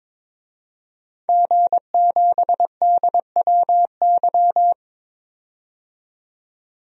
Morse code, heard as a keyed tone, about 22 wpm, 700 Hz. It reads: G7DWY
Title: DXCC-100 Call Signs - 5 Characters - 22wpm